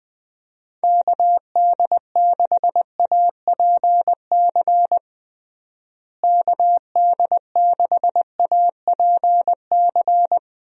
Morse code, heard as a keyed tone, about 20 wpm, 700 Hz. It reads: KD6APC KD6APC
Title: USA Call Signs - 6 Characters - Repeated 2x - 20wpm